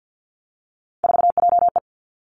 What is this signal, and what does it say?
Morse code, heard as a keyed tone, about 50 wpm, 700 Hz. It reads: 4JE
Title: Three Challenging Letter-Number Combinations 50wpm